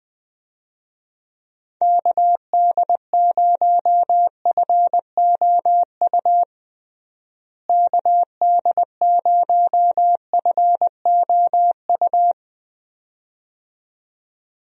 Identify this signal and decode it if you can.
Morse code, heard as a keyed tone, about 21 wpm, 700 Hz. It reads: KD0FOU KD0FOU
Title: USA Call Signs - 6 Characters - Repeated 2x - 20wpm